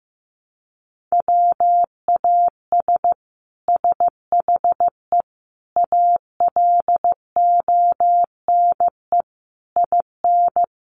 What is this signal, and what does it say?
Morse code, heard as a keyed tone, about 15 wpm, 700 Hz. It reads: WAS SHE ALONE IN
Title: Sentences from Top 1100 Words 15wpm